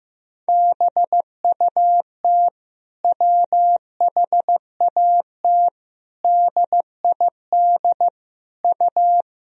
Morse code, heard as a keyed tone, about 15 wpm, 700 Hz. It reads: BUT WHAT DID U
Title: Sentences from Top 100 Words and Abbreviations 15wpm